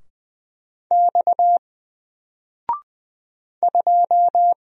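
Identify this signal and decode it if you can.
Morse code, heard as a keyed tone, about 20 wpm, 700 Hz. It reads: X 2
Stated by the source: Single letter and number Morse drill at 20 wpm, from N4LCD.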